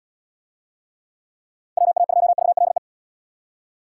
Morse code, heard as a keyed tone, about 70 wpm, 700 Hz. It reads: WA2VLE